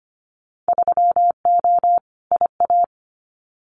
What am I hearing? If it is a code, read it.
Morse code, heard as a keyed tone, about 25 wpm, 700 Hz. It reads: 3O IA
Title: Winter Field Day Exchanges 25wpm